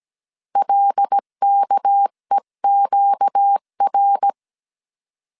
Morse code, heard as a keyed tone, about 17 wpm, 800 Hz. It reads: LKEQR
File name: CW / DET / 1 6 1500C S C 16wpm 2733